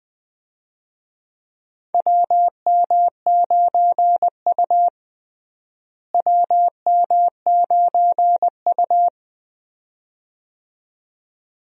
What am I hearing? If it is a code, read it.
Morse code, heard as a keyed tone, about 20 wpm, 700 Hz. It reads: WM9U WM9U
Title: USA Call Signs - 4 Characters - Repeated 2x - 20wpm